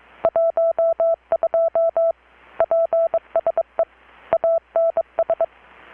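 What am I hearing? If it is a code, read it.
Morse code, heard as a keyed tone, about 23 wpm, 660 Hz. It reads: 12 PSE ANS
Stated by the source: Morse from KPH's Night of Nights XX maritime radiotelegraphy event, received in Chiapas, Mexico about 3,800 km away on a Kenwood R-600 with a grounded 10 m long-wire antenna.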